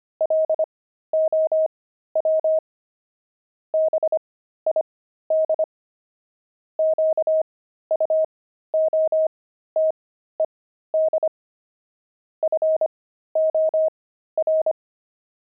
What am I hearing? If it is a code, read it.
Morse code, heard as a keyed tone, about 25 wpm, spaced as for 13 wpm, 630 Hz. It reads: LOW BID QUOTED FOR
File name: pangram-fs-13-cs-25-low